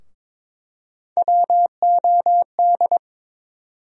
Morse code, heard as a keyed tone, about 22 wpm, 700 Hz. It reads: WOD
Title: USA Call Sign Postfixes - 3 Letters 22wpm